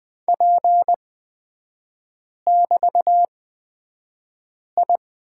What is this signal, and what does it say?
Morse code, heard as a keyed tone, about 20 wpm, spaced as for 5 wpm, 700 Hz. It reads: P<BT>I
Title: 255 Lesson 41e - Period - Three Characters 20wpm